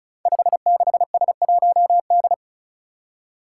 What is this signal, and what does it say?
Morse code, heard as a keyed tone, about 35 wpm, 700 Hz. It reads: 56S1D
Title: Five Challenging Letter-Number Combinations 35wpm